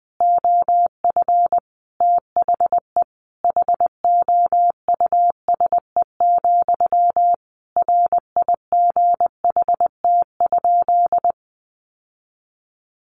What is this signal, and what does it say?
Morse code, heard as a keyed tone, about 20 wpm, 700 Hz. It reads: OF THE HOUSE, RIGHT?
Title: Sentences from Top 1200 Words 20wpm